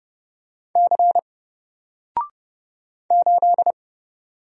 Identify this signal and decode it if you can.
Morse code, heard as a keyed tone, about 30 wpm, 700 Hz. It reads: C 8
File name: Single-Letter-Number-30wpm